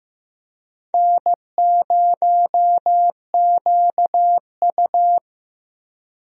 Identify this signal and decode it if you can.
Morse code, heard as a keyed tone, about 15 wpm, 700 Hz. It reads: N0QU
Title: USA Call Signs - 4 Characters - Repeated 2x - 15wpm